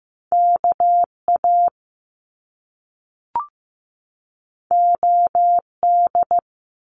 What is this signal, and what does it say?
Morse code, heard as a keyed tone, about 15 wpm, 700 Hz. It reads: KA OD
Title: Top Two Letter Combinations in Words 15wpm